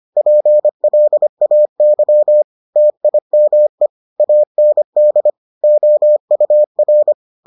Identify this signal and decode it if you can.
Morse code, heard as a keyed tone, about 25 wpm, 600 Hz. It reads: PLAY TIME AND OUR